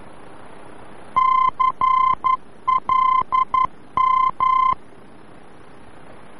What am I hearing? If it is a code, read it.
Morse code, heard as a keyed tone, about 11 wpm, 1040 Hz. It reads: CLM
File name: france / mx / CLM-VOR-F